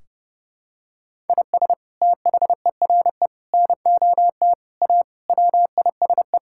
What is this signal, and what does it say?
Morse code, heard as a keyed tone, about 30 wpm, 700 Hz. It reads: IS THERE NOT A WISE
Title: Sentences from Top 1600 Words 30wpm